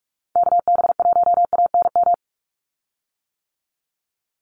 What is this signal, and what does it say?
Morse code, heard as a keyed tone, about 44 wpm, 700 Hz. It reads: KB1ANM
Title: USA Call Signs 45wpm